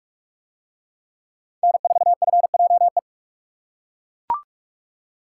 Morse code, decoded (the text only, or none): N4FJE